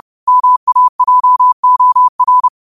Morse code, heard as a keyed tone, about 30 wpm, 1000 Hz. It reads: MAJOR
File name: qotdfiles / 30wpm